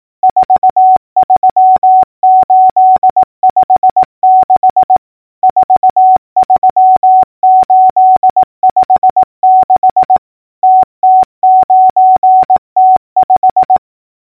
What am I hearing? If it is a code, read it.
Morse code, heard as a keyed tone, about 18 wpm, 750 Hz. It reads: 43856 43856 TT9T5
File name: numbers / V2-morse-clicks